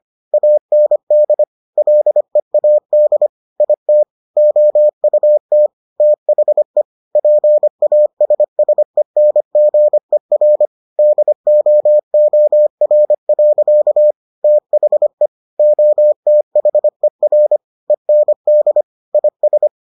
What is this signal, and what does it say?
Morse code, heard as a keyed tone, about 25 wpm, 600 Hz. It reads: AND LEAD IT OUT THE PASSENGER DOOR. THE OTHER END IS